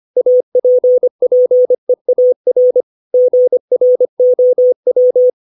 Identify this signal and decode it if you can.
Morse code, heard as a keyed tone, about 25 wpm, 500 Hz. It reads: APPEAR GROW
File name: 632 words / 632 Words 25WPM